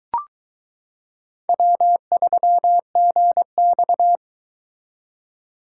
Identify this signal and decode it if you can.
Morse code, heard as a keyed tone, about 22 wpm, 700 Hz. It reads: W3GX